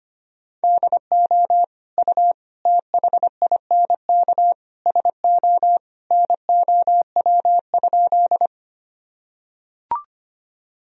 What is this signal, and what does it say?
Morse code, heard as a keyed tone, about 25 wpm, 700 Hz. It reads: DO U THINK SO NOW?
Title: Sentences from Top 200 Words and Abbreviations 25wpm